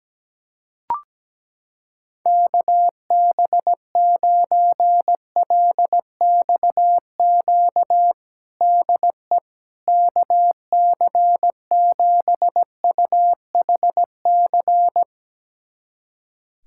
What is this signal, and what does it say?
Morse code, heard as a keyed tone, about 17 wpm, 700 Hz. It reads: KB9LXQ DE KC7UHC
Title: USA Callsign DE Callsign - 6 Characters - 17wpm